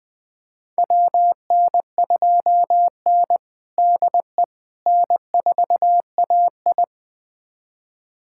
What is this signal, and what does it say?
Morse code, heard as a keyed tone, about 20 wpm, 700 Hz. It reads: WN2N DE N4AI
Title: USA Callsign DE Callsign - 4 Characters - 20wpm